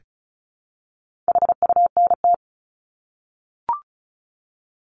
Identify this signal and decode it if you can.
Morse code, heard as a keyed tone, about 34 wpm, 700 Hz. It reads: HUNT
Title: Top 1501-1750 Words 35wpm